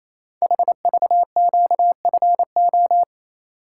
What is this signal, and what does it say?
Morse code, heard as a keyed tone, about 28 wpm, 700 Hz. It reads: HVQFO
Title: Code Groups - 5 Letters - 28wpm